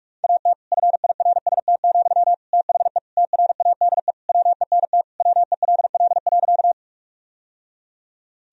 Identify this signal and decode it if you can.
Morse code, heard as a keyed tone, about 41 wpm, 700 Hz. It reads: AT FIRST, THE TRADE WENT WELL.
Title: Sentences from Top 1000 Words - Review Part 4 - 45wpm